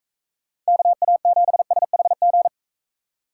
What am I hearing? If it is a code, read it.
Morse code, heard as a keyed tone, about 42 wpm, 700 Hz. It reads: KA7SHG